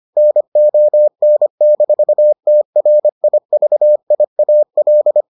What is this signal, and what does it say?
Morse code, heard as a keyed tone, about 25 wpm, 600 Hz. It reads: NON-TRIVIAL